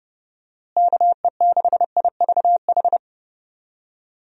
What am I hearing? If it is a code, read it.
Morse code, heard as a keyed tone, about 30 wpm, 700 Hz. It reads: KE6IVH
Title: USA Call Signs - 6 Characters - Repeated 2x - 30wpm